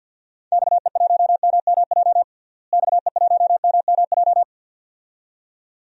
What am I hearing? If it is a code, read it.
Morse code, heard as a keyed tone, about 47 wpm, 700 Hz. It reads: XE1MMJ XE1MMJ
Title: DXCC-100 Call Signs - 6 Characters - Repeated 2x - 50wpm